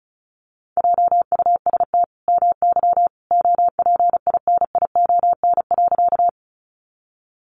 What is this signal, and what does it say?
Morse code, heard as a keyed tone, about 34 wpm, 700 Hz. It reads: JUST MY OPINION.
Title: Sentences from Top 1200 Words 35wpm